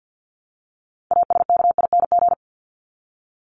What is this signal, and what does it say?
Morse code, heard as a keyed tone, about 48 wpm, 700 Hz. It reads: ASKING
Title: ING Suffix - Top 100 Words with Base Words 50wpm